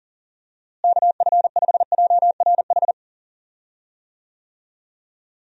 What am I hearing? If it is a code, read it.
Morse code, heard as a keyed tone, about 40 wpm, 700 Hz. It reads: KF5JRH
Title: USA Call Signs - 6 Characters - Repeated 2x - 40wpm